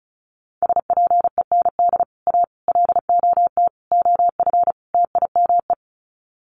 Morse code, heard as a keyed tone, about 34 wpm, 700 Hz. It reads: SPEND A LOT OF TIME
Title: Sets of 5 Words 35wpm